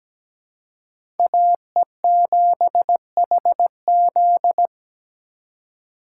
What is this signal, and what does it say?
Morse code, heard as a keyed tone, about 17 wpm, 700 Hz. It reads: AE7HZ
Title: USA Call Signs - 5 Characters - 3x Character Spacing - 17wpm